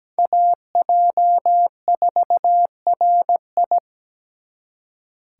Morse code, heard as a keyed tone, about 20 wpm, 700 Hz. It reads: AJ4RI
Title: Speed Racing - USA 2x2 Call Signs 17wpm